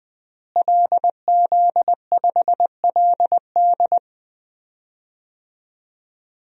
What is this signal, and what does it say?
Morse code, heard as a keyed tone, about 20 wpm, 700 Hz. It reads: LZ5LD
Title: DXCC-100 Call Signs - 5 Characters - Repeated 2x - 20wpm